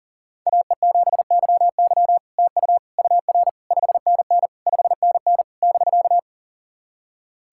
Morse code, heard as a keyed tone, about 40 wpm, 700 Hz. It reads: AE7YY TU UR 5NN 5NN <BK>
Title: SOTA QSO Practice 40wpm